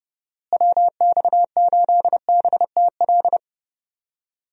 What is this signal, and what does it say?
Morse code, heard as a keyed tone, about 30 wpm, 700 Hz. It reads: WX8BTL
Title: USA Call Signs - 6 Characters - Repeated 3x - 30wpm